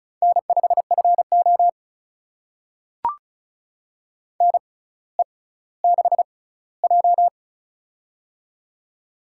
Morse code, heard as a keyed tone, about 36 wpm, spaced as for 18 wpm, 700 Hz. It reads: N5FO NE6J
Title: USA Call Signs - 6x Character Spacing - 35wpm